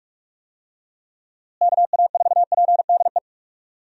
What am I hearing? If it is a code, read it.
Morse code, heard as a keyed tone, about 44 wpm, 700 Hz. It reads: KA4PDE
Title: USA Call Signs - 6 Characters - Repeated 2x - 45wpm